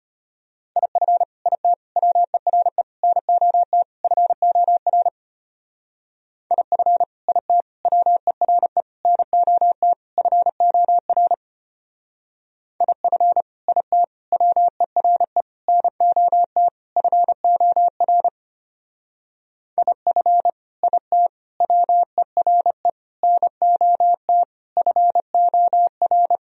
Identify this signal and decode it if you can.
Morse code, heard as a keyed tone, about 29 wpm, 700 Hz. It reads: IF IT WERE NOT FOR IF IT WERE NOT FOR IF IT WERE NOT FOR IF IT WERE NOT FOR
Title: Speed-Racing - Sets of 5 Words 25wpm